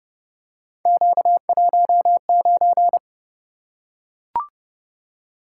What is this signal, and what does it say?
Morse code, heard as a keyed tone, about 29 wpm, 700 Hz. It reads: Q19